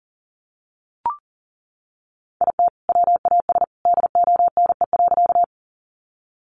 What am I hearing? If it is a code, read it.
Morse code, heard as a keyed tone, about 40 wpm, 700 Hz. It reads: IT WAS DONE.
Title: Sentences from Top 500 Words 40wpm